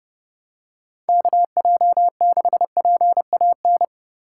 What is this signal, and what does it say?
Morse code, heard as a keyed tone, about 31 wpm, 700 Hz. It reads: KJ6PAN